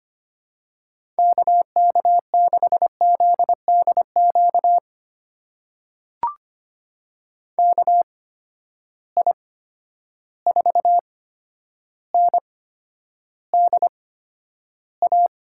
Morse code, heard as a keyed tone, about 25 wpm, spaced as for 11 wpm, 700 Hz. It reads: KK6ZDQ KI4NDA
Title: USA Call Signs - 6 Characters - 8x Character Spacing - 25wpm